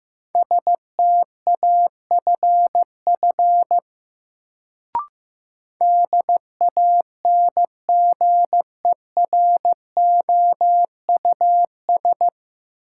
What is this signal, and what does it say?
Morse code, heard as a keyed tone, about 15 wpm, 700 Hz. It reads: STAFF DANGEROUS
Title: Sentences from Top 1200 Words - New Word Drill - 15wpm